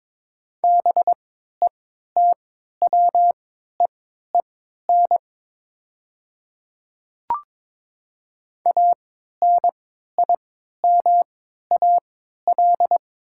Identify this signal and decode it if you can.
Morse code, heard as a keyed tone, about 23 wpm, spaced as for 13 wpm, 700 Hz. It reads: BETWEEN ANIMAL
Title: 3x Character Spacing - Top 300 Words - 22wpm